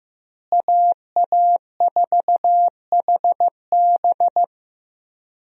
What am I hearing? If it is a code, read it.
Morse code, heard as a keyed tone, about 15 wpm, 700 Hz. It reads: AA4HB
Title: USA Call Signs - 5 Characters - Repeated 2x - 15wpm